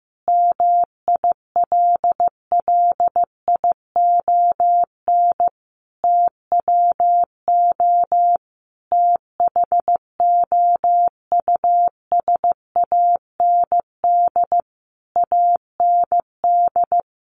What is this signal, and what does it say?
Morse code, heard as a keyed tone, about 15 wpm, 700 Hz. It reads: MILLION TWO THOUSAND AND
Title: A Sound Of Thunder 15wpm